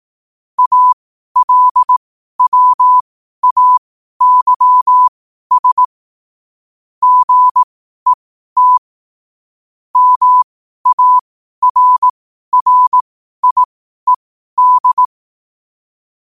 Morse code, heard as a keyed tone, about 18 wpm, spaced as for 12 wpm, 1000 Hz. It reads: ALWAYS GET MARRIED
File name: qotdfiles / 12wpm